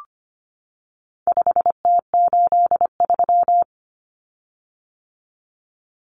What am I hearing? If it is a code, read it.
Morse code, heard as a keyed tone, about 25 wpm, 700 Hz. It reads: 5T83
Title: Common Cut Numbers - 4-Digit 25wpm